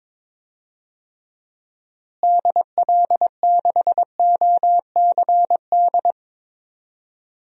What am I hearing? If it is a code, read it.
Morse code, heard as a keyed tone, about 22 wpm, 700 Hz. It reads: DL6OCD
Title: DXCC-100 Call Signs - 6 Characters - Repeated 3x - 22wpm